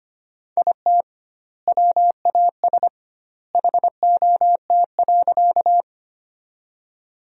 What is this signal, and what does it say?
Morse code, heard as a keyed tone, about 25 wpm, 700 Hz. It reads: IT WAS HOT.
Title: 2x Word-Spacing - Sentences from Top 500 Words - 25wpm